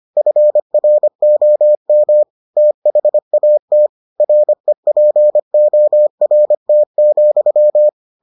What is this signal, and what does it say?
Morse code, heard as a keyed tone, about 25 wpm, 600 Hz. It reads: FROM THAT REPORT,